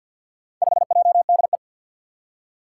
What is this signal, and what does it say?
Morse code, heard as a keyed tone, about 47 wpm, 700 Hz. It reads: 5JDE
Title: Four Challenging Letter-Number Combinations 50wpm